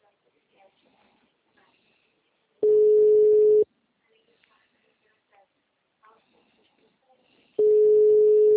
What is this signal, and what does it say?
Ringback tone (Europe, 425 Hz), with speech.